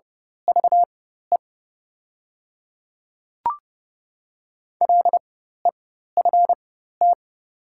Morse code, heard as a keyed tone, about 31 wpm, spaced as for 14 wpm, 690 Hz. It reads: VE LEFT
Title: 4x Character Spacing - Top 500 Words - 30wpm